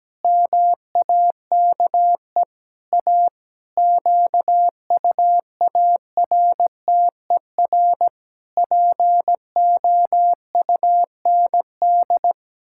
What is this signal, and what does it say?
Morse code, heard as a keyed tone, about 17 wpm, 700 Hz. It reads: MAKE A QUARTER POUND
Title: Pangram Challenge v2 - 17wpm